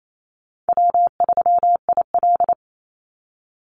Morse code, heard as a keyed tone, about 28 wpm, 700 Hz. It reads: W3IL
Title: Easy Call Signs Repeated 3x - 28wpm